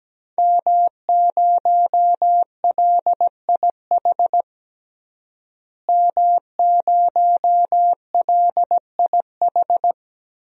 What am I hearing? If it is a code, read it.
Morse code, heard as a keyed tone, about 17 wpm, 700 Hz. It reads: M0LIH M0LIH
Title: DXCC-100 Call Signs - 5 Characters - Repeated 2x - 17wpm